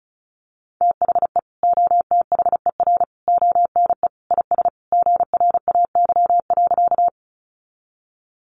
Morse code, heard as a keyed tone, about 34 wpm, 700 Hz. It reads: THE OTHER ONE IS GRAY.
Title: Additional Easy Phrases 35wpm